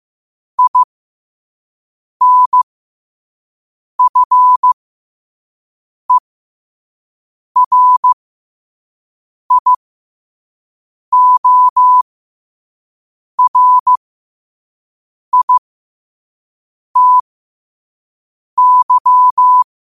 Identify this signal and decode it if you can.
Morse code, heard as a keyed tone, about 15 wpm, spaced as for 5 wpm, 1000 Hz. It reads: INFERIORITY
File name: qotdfiles / 5wpm